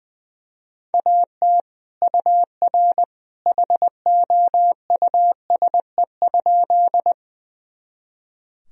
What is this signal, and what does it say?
Morse code, heard as a keyed tone, about 20 wpm, 700 Hz. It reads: AT UR HOUSE?
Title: Sentences from Top 200 Words and Abbreviations 20wpm